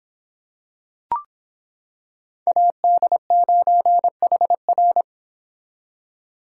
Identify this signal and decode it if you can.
Morse code, heard as a keyed tone, about 25 wpm, 700 Hz. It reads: AD9HR